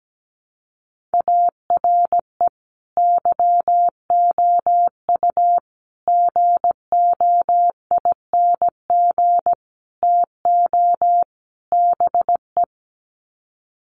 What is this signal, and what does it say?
Morse code, heard as a keyed tone, about 17 wpm, 700 Hz. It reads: ARE YOU GOING TO BE
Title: Sets of 5 Words Encore 17wpm